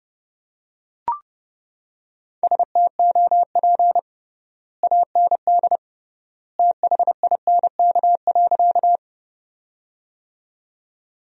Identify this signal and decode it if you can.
Morse code, heard as a keyed tone, about 32 wpm, 700 Hz. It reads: STOP AND THINK.
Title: 3x Word-Spacing - Sentences from Top 300 Words - 30wpm